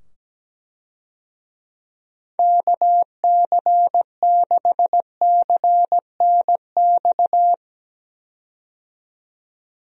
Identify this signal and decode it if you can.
Morse code, heard as a keyed tone, about 17 wpm, 700 Hz. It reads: KC6CNX
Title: USA Call Signs - 6 Characters - 17wpm